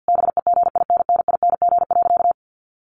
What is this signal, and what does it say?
Morse code, heard as a keyed tone, about 50 wpm, 700 Hz. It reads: BEGINNING.
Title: A Sound Of Thunder 50wpm